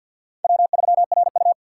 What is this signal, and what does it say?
Morse code, heard as a keyed tone, about 52 wpm, 700 Hz. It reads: W3RU
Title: USA Call Signs - 4 Characters - 2x Character Spacing - 50wpm